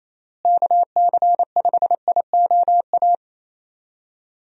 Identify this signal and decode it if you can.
Morse code, heard as a keyed tone, about 28 wpm, 700 Hz. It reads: KC5IOA
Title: USA Call Signs - 6 Characters - Repeated 3x - 28wpm